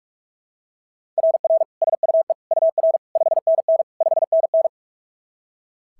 Morse code, heard as a keyed tone, about 46 wpm, 690 Hz. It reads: RR SUE UR 5NN 5NN
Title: SOTA QSO Practice 45wpm